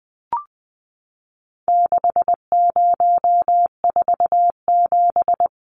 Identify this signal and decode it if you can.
Morse code, heard as a keyed tone, about 20 wpm, 700 Hz. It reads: 6047
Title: Numbers - 4-Digit 20wpm